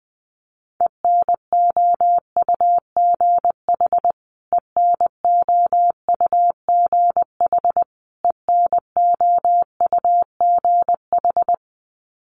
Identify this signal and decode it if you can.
Morse code, heard as a keyed tone, about 20 wpm, 700 Hz. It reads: ENOUGH ENOUGH ENOUGH
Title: Top 500 Words Repeated 20wpm